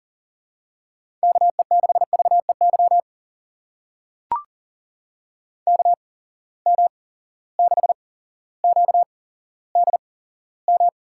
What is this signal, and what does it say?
Morse code, heard as a keyed tone, about 43 wpm, spaced as for 17 wpm, 700 Hz. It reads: KE6VEY KM6QDM
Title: USA Call Signs - 6 Characters - 8x Character Spacing - 40wpm